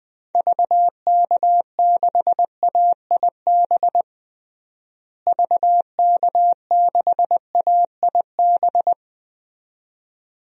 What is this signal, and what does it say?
Morse code, heard as a keyed tone, about 20 wpm, 700 Hz. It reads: VK6AIB VK6AIB
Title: DXCC-100 Call Signs - 6 Characters - Repeated 2x - 20wpm